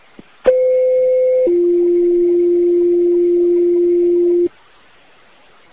Two-tone page, heard as a one-second tone, then a three-second tone, about 510 then 350 Hz. It is heard from 0.5 to 4.5 s.